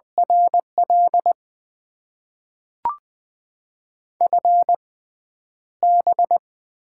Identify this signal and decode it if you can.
Morse code, heard as a keyed tone, about 20 wpm, spaced as for 7 wpm, 700 Hz. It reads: R L FB
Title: Top 20 QSO Elements - 6x Character Spacing - 20wpm